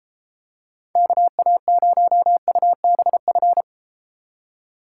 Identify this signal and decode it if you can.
Morse code, heard as a keyed tone, about 33 wpm, 700 Hz. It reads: KA0UBF